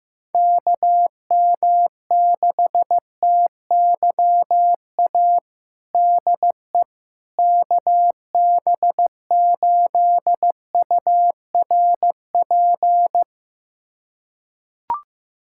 Morse code, heard as a keyed tone, about 15 wpm, 700 Hz. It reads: KM6TYA DE KB8URP
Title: USA Callsign DE Callsign 15wpm